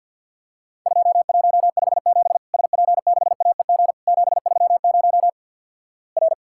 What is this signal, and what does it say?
Morse code, heard as a keyed tone, about 50 wpm, 700 Hz. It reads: 2157 SP6AEG 730 R
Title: Worked All Europe DX Contest - QTC Exchange - 50wpm